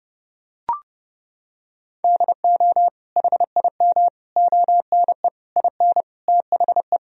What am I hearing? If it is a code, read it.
Morse code, heard as a keyed tone, about 32 wpm, 700 Hz. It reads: DO HIM ONE IN THE
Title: Sentences from Top 1000 Words - Review Part 1 - 30wpm